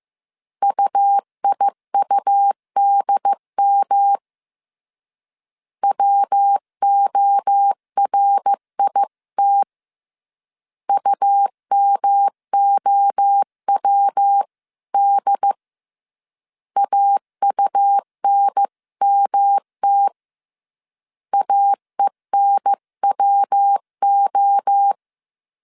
Morse code, heard as a keyed tone, about 15 wpm, 800 Hz. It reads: UIUDM WORIT UMOWD AUNMT AENWO